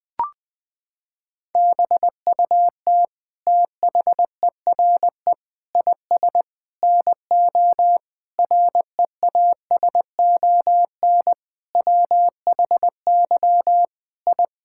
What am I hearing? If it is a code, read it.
Morse code, heard as a keyed tone, about 20 wpm, 700 Hz. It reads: BUT THERE IS NO REASON WHY I
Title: Sentences from Top 1000 Words - Review Part 6 - 20wpm